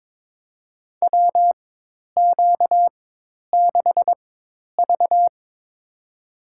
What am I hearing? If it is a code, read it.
Morse code, heard as a keyed tone, about 22 wpm, spaced as for 10 wpm, 700 Hz. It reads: WQ6V
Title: USA Call Signs - 4x Character Spacing - 22wpm